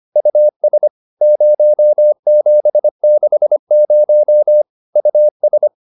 Morse code, heard as a keyed tone, about 25 wpm, 600 Hz. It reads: US 0760 US